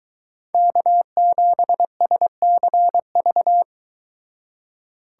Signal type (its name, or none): Morse code